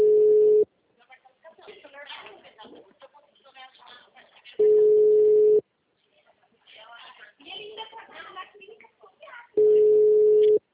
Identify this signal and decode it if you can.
Ringback tone (Europe, 425 Hz), with speech.
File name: out- -2001- - - .33218